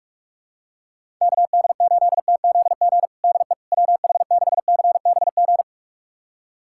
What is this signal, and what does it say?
Morse code, heard as a keyed tone, about 45 wpm, 700 Hz. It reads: KD8TZG DE WH6CBG
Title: USA Callsign DE Callsign - 6 Characters - 45wpm